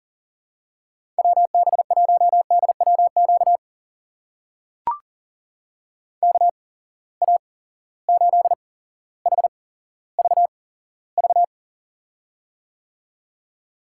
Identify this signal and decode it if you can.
Morse code, heard as a keyed tone, about 37 wpm, spaced as for 17 wpm, 700 Hz. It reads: WB1DWQ KA8HVV
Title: USA Call Signs - 6 Characters - 8x Character Spacing - 40wpm